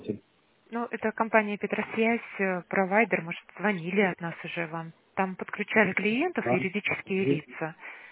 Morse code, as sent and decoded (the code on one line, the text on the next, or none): none